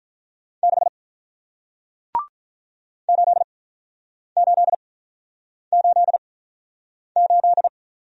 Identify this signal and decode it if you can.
Morse code, heard as a keyed tone, about 48 wpm, spaced as for 9 wpm, 700 Hz. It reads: 6 8888